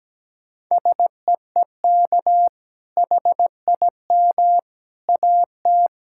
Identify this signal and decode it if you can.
Morse code, heard as a keyed tone, about 17 wpm, 700 Hz. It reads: SEEK HIM AT
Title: Sentences from Top 1600 Words 17wpm